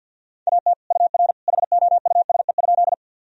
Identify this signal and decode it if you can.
Morse code, heard as a keyed tone, about 52 wpm, 700 Hz. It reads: AT UR HOUSE?
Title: Sentences from Top 200 Words and Abbreviations 50wpm